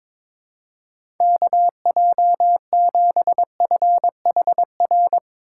Morse code, heard as a keyed tone, about 23 wpm, 700 Hz. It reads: KJ7FHR
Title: USA Call Signs - 6 Characters - 3x Character Spacing - 22wpm